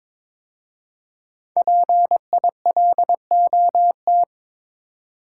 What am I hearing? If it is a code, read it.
Morse code, heard as a keyed tone, about 22 wpm, 700 Hz. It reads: PILOT
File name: 5 Letter Words - First 3 Letters Given - 22wpm